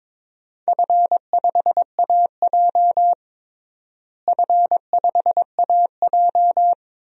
Morse code, heard as a keyed tone, about 22 wpm, 700 Hz. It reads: F5AJ F5AJ
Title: DXCC-100 Call Signs - 4 Characters - Repeated 3x - 22wpm